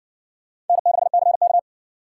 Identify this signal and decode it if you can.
Morse code, heard as a keyed tone, about 60 wpm, 700 Hz. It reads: N6CK